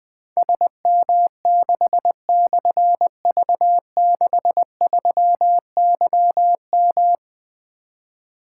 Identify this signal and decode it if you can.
Morse code, heard as a keyed tone, about 20 wpm, 700 Hz. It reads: SM6/V63YM
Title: Difficult Call Signs 20wpm